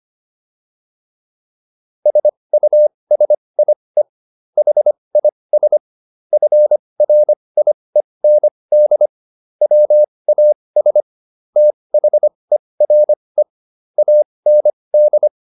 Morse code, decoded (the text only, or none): SUSIE HIS FRIEND WAS THERE AND